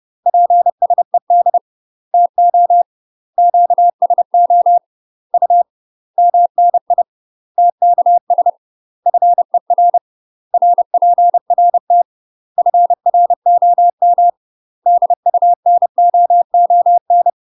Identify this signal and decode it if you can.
Morse code, heard as a keyed tone, about 30 wpm, 700 Hz. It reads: PSED TO QSO U MNI TKS FER RPRT FROM DUNOON
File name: UK MASTER QSOs TRAINING / 30WPM / UK MASTER QSOs 0047